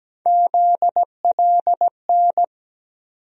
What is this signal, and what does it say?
Morse code, heard as a keyed tone, about 17 wpm, 700 Hz. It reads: ZLN
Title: USA Call Sign Postfixes - 3 Letters 17wpm